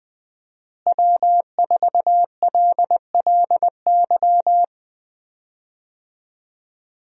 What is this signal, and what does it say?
Morse code, heard as a keyed tone, about 20 wpm, 700 Hz. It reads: W4LLY
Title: USA Call Signs - 5 Characters - Repeated 2x - 20wpm